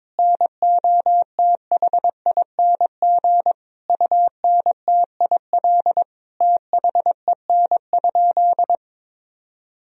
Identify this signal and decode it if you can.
Morse code, heard as a keyed tone, about 22 wpm, 700 Hz. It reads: NOTHING UNTIL THEN?
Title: Sentences from Top 500 Words and Abbreviations 22wpm